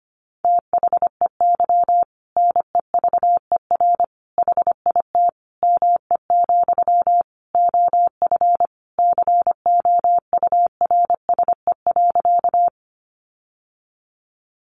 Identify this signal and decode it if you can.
Morse code, heard as a keyed tone, about 25 wpm, 700 Hz. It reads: THEY NEVER HIT ME, OF COURSE.
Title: Sentences from Top 1200 Words 25wpm